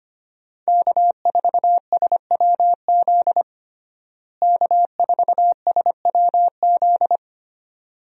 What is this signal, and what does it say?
Morse code, heard as a keyed tone, about 25 wpm, 700 Hz. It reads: K4SWZ K4SWZ
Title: USA Call Signs - 5 Characters - Repeated 2x - 25wpm